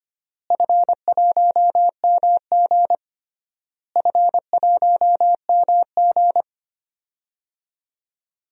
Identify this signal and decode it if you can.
Morse code, heard as a keyed tone, about 25 wpm, 700 Hz. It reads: F1MG F1MG
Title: DXCC-100 Call Signs - 4 Characters - Repeated 3x - 25wpm